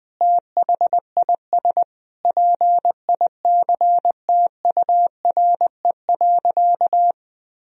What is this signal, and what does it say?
Morse code, heard as a keyed tone, about 21 wpm, 700 Hz. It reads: THIS PICTURE.
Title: Sentences from Top 1000 Words - Review Part 1 - 20wpm